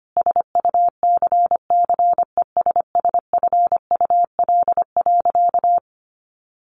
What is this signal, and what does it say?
Morse code, heard as a keyed tone, about 25 wpm, 700 Hz. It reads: SUCCESSFUL.
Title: Sentences from Top 1300 Words 25wpm